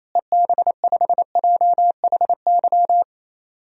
Morse code, heard as a keyed tone, about 28 wpm, 700 Hz. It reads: EB5JHY
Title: DXCC-100 Call Signs - 6 Characters - Repeated 3x - 28wpm